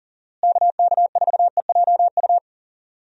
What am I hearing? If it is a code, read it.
Morse code, heard as a keyed tone, about 40 wpm, 700 Hz. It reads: KK4EJU
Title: USA Call Signs - 6 Characters - Repeated 3x - 40wpm